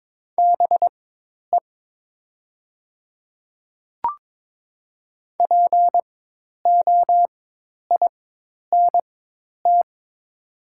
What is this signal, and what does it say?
Morse code, heard as a keyed tone, about 22 wpm, spaced as for 10 wpm, 700 Hz. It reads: BE POINT
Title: 4x Character Spacing - Top 200 Words - 22wpm